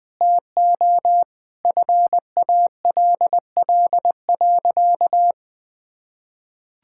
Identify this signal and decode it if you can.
Morse code, heard as a keyed tone, about 19 wpm, 700 Hz. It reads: TO FALL.
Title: Sentences from Top 1000 Words - Review Part 6 - 20wpm